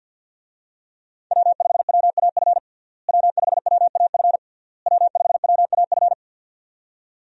Morse code, heard as a keyed tone, about 52 wpm, 700 Hz. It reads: W5WAF W5WAF W5WAF
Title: USA Call Signs - 5 Characters - Repeated 3x - 50wpm